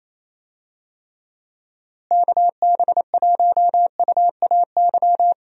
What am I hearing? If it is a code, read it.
Morse code, heard as a keyed tone, about 27 wpm, 700 Hz. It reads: KB1UAY
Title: USA Call Signs - 6 Characters - Repeated 2x - 28wpm